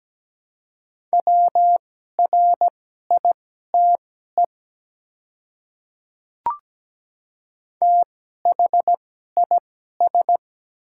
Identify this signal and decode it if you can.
Morse code, heard as a keyed tone, about 17 wpm, spaced as for 12 wpm, 700 Hz. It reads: WRITE THIS
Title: 2x Character Spacing - Top 100 Words - 17wpm